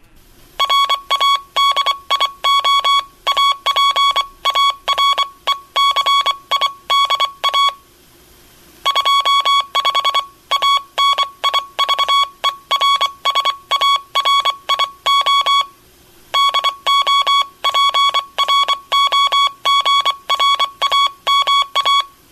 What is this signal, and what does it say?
Morse code, heard as a keyed tone, about 24 wpm, 1120 Hz. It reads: RADIO APARECIDA 25 ANIVERSARIO DO PROGRAMA
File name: Aniversario 25 anos Encontro Dx- Radio aparecida